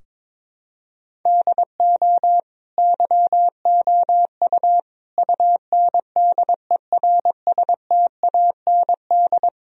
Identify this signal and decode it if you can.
Morse code, heard as a keyed tone, about 21 wpm, 700 Hz. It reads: DO YOU UNDERSTAND
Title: Sentences from Top 1000 Words - Review Part 3 - 22wpm